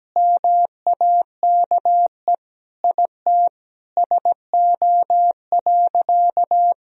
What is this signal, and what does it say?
Morse code, heard as a keyed tone, about 17 wpm, 700 Hz. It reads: MAKE IT SO.
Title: Sentences from Top 1000 Words - Review Part 3 - 17wpm